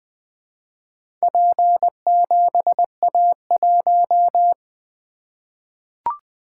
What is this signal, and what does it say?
Morse code, heard as a keyed tone, about 20 wpm, 700 Hz. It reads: P7A1
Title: Four Letter-Number Combinations - 20wpm